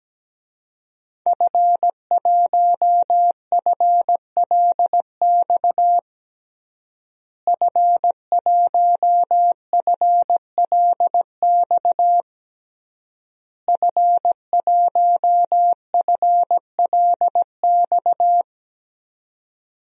Morse code, heard as a keyed tone, about 17 wpm, 700 Hz. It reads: F1FLX F1FLX F1FLX
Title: DXCC-100 Call Signs - 5 Characters - Repeated 3x - 17wpm